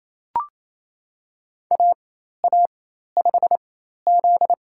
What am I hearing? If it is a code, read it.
Morse code, heard as a keyed tone, about 28 wpm, spaced as for 13 wpm, 700 Hz. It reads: AA5Z
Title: USA Call Signs - 4 Characters - 4x Character Spacing - 28wpm